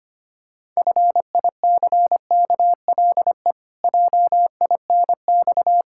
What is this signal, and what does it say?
Morse code, heard as a keyed tone, about 25 wpm, 700 Hz. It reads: FICKLE JINX
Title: Pangram Challenge v2 - 25wpm